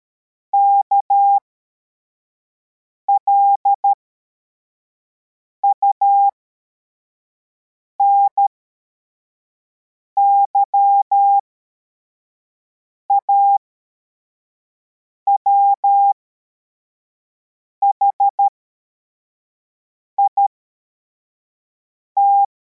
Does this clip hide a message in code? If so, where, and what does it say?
Yes: Morse code from 0.5 to 22.4 s, heard as a keyed tone, about 13 wpm, spaced as for 4 wpm, 800 Hz. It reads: KLUNY AWHIT